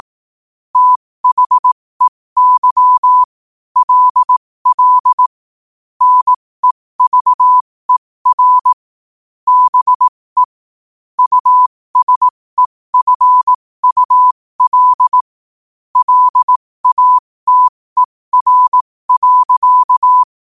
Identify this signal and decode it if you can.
Morse code, heard as a keyed tone, about 18 wpm, spaced as for 15 wpm, 1000 Hz. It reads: THEY LL NEVER BE USEFUL LATER.
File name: qotdfiles / 15wpm